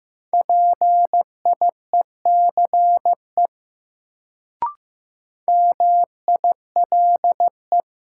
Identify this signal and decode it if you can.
Morse code, heard as a keyed tone, about 15 wpm, 700 Hz. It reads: PIECE MILE
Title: Sentences from Top 400 Words - New Word Drill - 15wpm